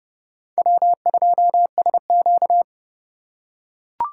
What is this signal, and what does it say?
Morse code, heard as a keyed tone, about 28 wpm, 700 Hz. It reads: W2SQ